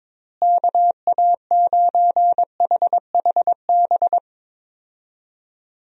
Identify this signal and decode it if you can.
Morse code, heard as a keyed tone, about 22 wpm, 700 Hz. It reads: KA9HHB
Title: USA Call Signs - 6 Characters - Repeated 3x - 22wpm